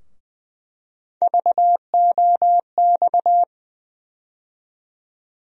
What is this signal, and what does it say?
Morse code, heard as a keyed tone, about 20 wpm, 700 Hz. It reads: VOX
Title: USA Call Sign Postfixes - 3 Letters 20wpm